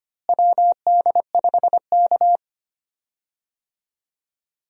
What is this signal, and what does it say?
Morse code, heard as a keyed tone, about 25 wpm, 700 Hz. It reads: WD5K
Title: USA Call Signs - 4 Characters - Repeated 2x - 25wpm